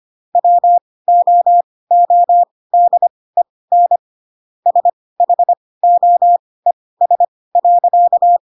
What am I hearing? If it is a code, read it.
Morse code, heard as a keyed tone, about 25 wpm, spaced as for 18 wpm, 700 Hz. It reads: WOODEN SHOES.
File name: Pinocchio ch3 12 25 ik1hge